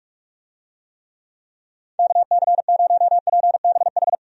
Morse code, heard as a keyed tone, about 44 wpm, 700 Hz. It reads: KC0PBH
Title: USA Call Signs - 6 Characters - 45wpm